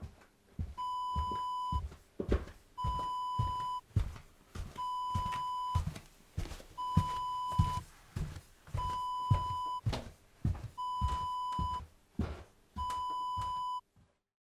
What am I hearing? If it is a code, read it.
There is no code.